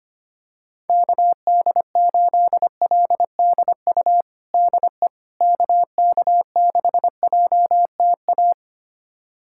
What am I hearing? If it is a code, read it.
Morse code, heard as a keyed tone, about 25 wpm, 700 Hz. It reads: KD8LDU DE KK6JTA
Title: USA Callsign DE Callsign - 6 Characters - 25wpm